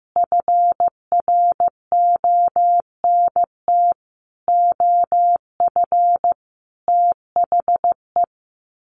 Morse code, heard as a keyed tone, about 14 wpm, 700 Hz. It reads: FRONT OF THE
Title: Sets of 5 Words 15wpm